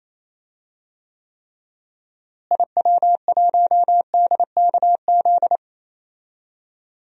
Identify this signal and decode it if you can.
Morse code, heard as a keyed tone, about 28 wpm, 700 Hz. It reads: IW1DKZ
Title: DXCC-100 Call Signs - 6 Characters - Repeated 3x - 28wpm